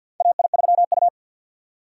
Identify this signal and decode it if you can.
Morse code, heard as a keyed tone, about 52 wpm, 700 Hz. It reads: AI3U
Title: USA Call Signs - 4 Characters - 2x Character Spacing - 50wpm